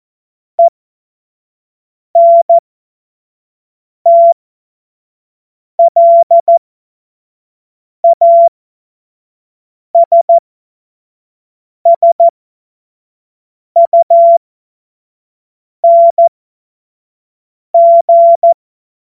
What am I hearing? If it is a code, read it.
Morse code, heard as a keyed tone, about 14 wpm, spaced as for 5 wpm, 680 Hz. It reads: ENTLASSUNG
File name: Morsekurs / Witze / Witz 15. November 2022 25 F70